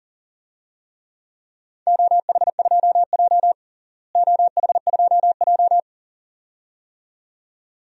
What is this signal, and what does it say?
Morse code, heard as a keyed tone, about 40 wpm, 700 Hz. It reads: OH2J OH2J
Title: DXCC-100 Call Signs - 4 Characters - Repeated 2x - 40wpm